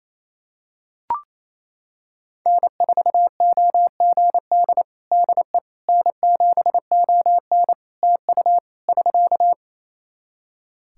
Morse code, heard as a keyed tone, about 28 wpm, 700 Hz. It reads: N4OGD DE N7ON TU <SK>
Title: Practice QSO - Exchange 4 - Ending - 28wpm